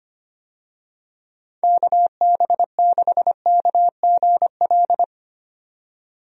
Morse code, heard as a keyed tone, about 25 wpm, 700 Hz. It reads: KB6KGL
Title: USA Call Signs - 6 Characters - 25wpm